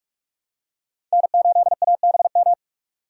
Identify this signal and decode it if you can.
Morse code, heard as a keyed tone, about 44 wpm, 700 Hz. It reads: N8ABM
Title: USA Call Signs - 5 Characters - 2x Character Spacing - 45wpm